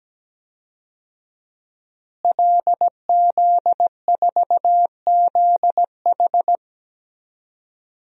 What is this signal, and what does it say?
Morse code, heard as a keyed tone, about 17 wpm, 700 Hz. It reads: LZ4ZH
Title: DXCC-100 Call Signs - 5 Characters - Repeated 3x - 17wpm